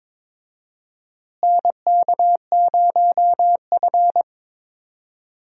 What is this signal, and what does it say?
Morse code, heard as a keyed tone, about 21 wpm, 700 Hz. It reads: NK0F
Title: USA Call Signs - 4 Characters - Repeated 3x - 22wpm